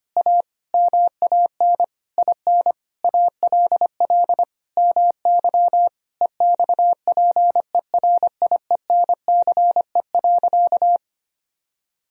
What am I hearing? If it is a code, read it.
Morse code, heard as a keyed tone, about 25 wpm, 700 Hz. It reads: A MAN IN ALL MY EXPERIENCE.
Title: Sentences from Top 1000 Words - Review Part 3 - 25wpm